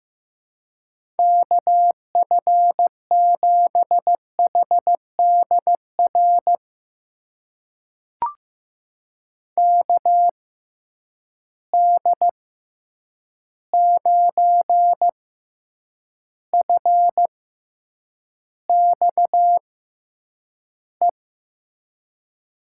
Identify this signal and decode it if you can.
Morse code, heard as a keyed tone, about 16 wpm, spaced as for 8 wpm, 700 Hz. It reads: KF7HDR KD9FXE
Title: USA Call Signs - 6 Characters - 6x Character Spacing - 15wpm